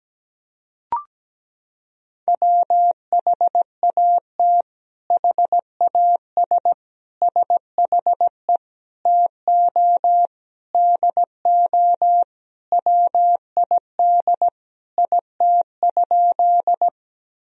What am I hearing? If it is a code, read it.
Morse code, heard as a keyed tone, about 17 wpm, 700 Hz. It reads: WHAT HAS SHE TO DO WID IT?
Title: Sentences from Top 100 Words and Abbreviations 17wpm